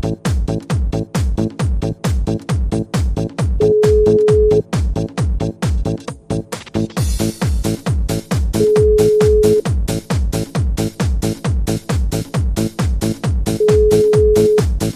Ringback tone (Europe, 425 Hz).